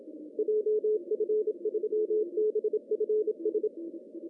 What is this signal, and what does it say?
Morse code, heard as a keyed tone, about 26 wpm, 430 Hz, with radio noise. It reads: JF3BFS
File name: CW / cqww-cw-k5tr